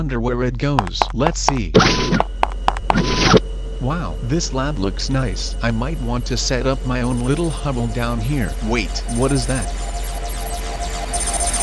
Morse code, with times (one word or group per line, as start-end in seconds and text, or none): none